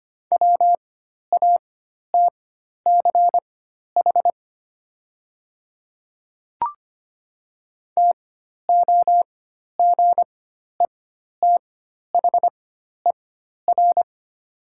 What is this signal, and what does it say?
Morse code, heard as a keyed tone, about 25 wpm, spaced as for 12 wpm, 700 Hz. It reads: WATCH TOGETHER
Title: 4x Character Spacing - Top 300 Words - 25wpm